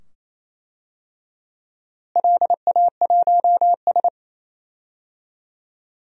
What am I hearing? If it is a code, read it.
Morse code, heard as a keyed tone, about 27 wpm, 700 Hz. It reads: LA1S
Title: DXCC-100 Call Signs - 4 Characters - Repeated 2x - 28wpm